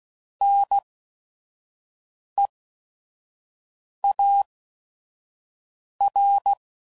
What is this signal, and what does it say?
Morse code, heard as a keyed tone, about 16 wpm, spaced as for 5 wpm, 800 Hz. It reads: NEAR
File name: Prefixes / By First Letter / Ns